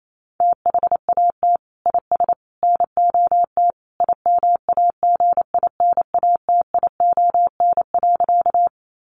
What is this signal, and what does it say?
Morse code, heard as a keyed tone, about 28 wpm, 700 Hz. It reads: THAT IS NOT IMAGINATION.
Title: Sentences from Top 1300 Words 28wpm